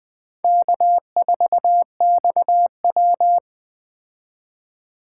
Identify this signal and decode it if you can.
Morse code, heard as a keyed tone, about 20 wpm, 700 Hz. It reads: K4XW
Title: Speed Racing - USA 1x2 Call Signs 17wpm